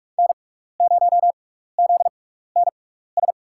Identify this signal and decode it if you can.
Morse code, heard as a keyed tone, about 41 wpm, spaced as for 15 wpm, 700 Hz. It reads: N0ZNS
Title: USA Call Signs - 6x Character Spacing - 45wpm